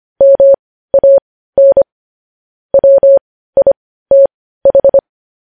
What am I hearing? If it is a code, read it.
Morse code, heard as a keyed tone, about 25 wpm, spaced as for 15 wpm, 550 Hz. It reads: MAN WITH
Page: News Headlines in Morse Code at 15 WPM